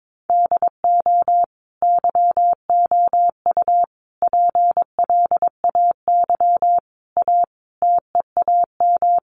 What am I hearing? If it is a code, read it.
Morse code, heard as a keyed tone, about 22 wpm, 700 Hz. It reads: DO YOU PLAY A TEAM
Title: Sentences from Top 1500 Words 22wpm